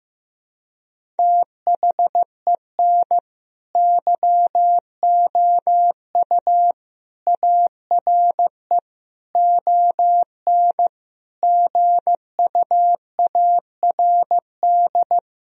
Morse code, heard as a keyed tone, about 15 wpm, 700 Hz. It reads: THEN YOU ARE ON GUARD
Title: Sentences from Top 1000 Words - Review Part 4 - 15wpm